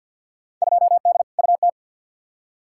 Morse code, heard as a keyed tone, about 52 wpm, 700 Hz. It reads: 2D UT
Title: Field Day Exchanges Encore - 50wpm